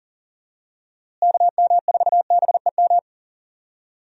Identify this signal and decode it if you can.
Morse code, heard as a keyed tone, about 40 wpm, 700 Hz. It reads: KM4BEM
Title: USA Call Signs - 6 Characters - Repeated 2x - 40wpm